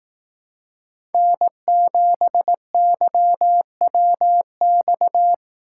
Morse code, heard as a keyed tone, about 18 wpm, 700 Hz. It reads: N7YWX
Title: Speed Racing - USA 1x3 Call Signs 15wpm